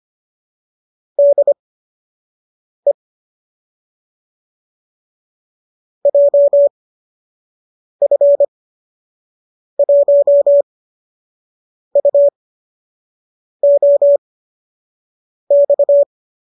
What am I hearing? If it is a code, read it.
Morse code, heard as a keyed tone, about 25 wpm, spaced as for 6 wpm, 570 Hz. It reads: DE JF1UOX